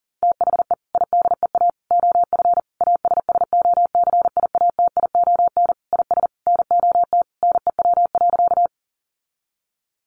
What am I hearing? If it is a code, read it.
Morse code, heard as a keyed tone, about 40 wpm, 700 Hz. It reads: THE IDEA OF ASSOCIATION IS NOT NEW.
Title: Sentences from Top 1100 Words 40wpm